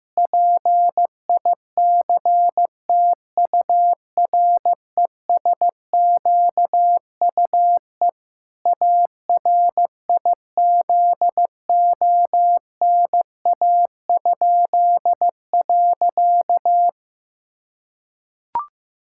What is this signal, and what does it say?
Morse code, heard as a keyed tone, about 16 wpm, 700 Hz. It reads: PICTURESQUE ARIZONA?.
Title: Pangram Challenge v2 - 15wpm